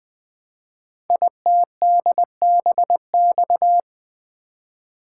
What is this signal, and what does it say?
Morse code, heard as a keyed tone, about 20 wpm, 700 Hz. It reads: ITDBX
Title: Code Groups - 5 Letters - 20wpm